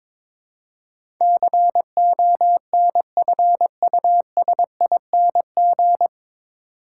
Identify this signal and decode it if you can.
Morse code, heard as a keyed tone, about 22 wpm, 700 Hz. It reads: CONFUSING
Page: Morse Code Ninja practice 22wpm